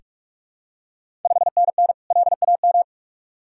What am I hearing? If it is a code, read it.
Morse code, heard as a keyed tone, about 46 wpm, 700 Hz. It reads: 5NN LAM
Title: Ontario QSO Party Exchange 45wpm